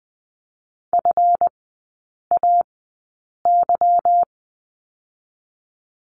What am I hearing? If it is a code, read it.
Morse code, heard as a keyed tone, about 20 wpm, spaced as for 8 wpm, 700 Hz. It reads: FAY
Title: 145 Lesson 24e - Letter y - Three Characters 20wpm